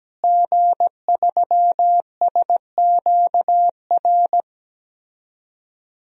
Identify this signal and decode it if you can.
Morse code, heard as a keyed tone, about 17 wpm, 700 Hz. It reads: G3SQR
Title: DXCC-100 Call Signs - 5 Characters - Repeated 2x - 17wpm